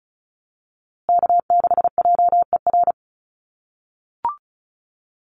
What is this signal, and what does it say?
Morse code, heard as a keyed tone, about 34 wpm, 700 Hz. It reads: K6JER
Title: USA Call Signs Repeated 3x 35wpm